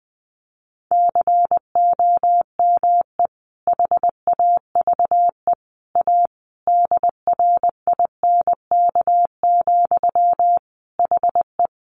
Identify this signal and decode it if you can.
Morse code, heard as a keyed tone, about 20 wpm, 700 Hz. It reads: COME HAVE A DRINK, HE
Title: Sentences from Top 1500 Words 20wpm